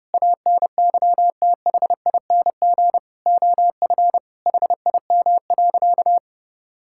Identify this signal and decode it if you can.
Morse code, heard as a keyed tone, about 30 wpm, 700 Hz. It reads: ANYTHING OF HIM.
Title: Sentences from Top 1600 Words 30wpm